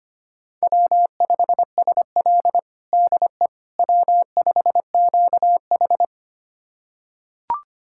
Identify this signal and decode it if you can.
Morse code, heard as a keyed tone, about 25 wpm, 700 Hz. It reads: W5SL DE W5QH
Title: USA Callsign DE Callsign - 4 Characters - 25wpm